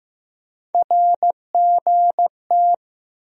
Morse code, heard as a keyed tone, about 15 wpm, 700 Hz. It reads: RGT
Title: USA Call Sign Postfixes - 3 Letters 15wpm